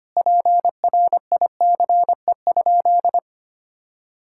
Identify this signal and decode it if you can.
Morse code, heard as a keyed tone, about 25 wpm, 700 Hz. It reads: PRICE?
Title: Sentences from Top 1600 Words 25wpm